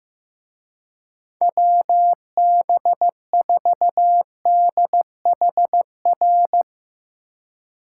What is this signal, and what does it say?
Morse code, heard as a keyed tone, about 16 wpm, 700 Hz. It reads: WB4DHR
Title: Speed Racing - USA 2x3 Call Signs 15wpm